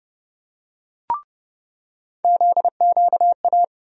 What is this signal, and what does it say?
Morse code, heard as a keyed tone, about 30 wpm, 700 Hz. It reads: ZQA
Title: USA Call Sign Postfixes - 3 Letters 30wpm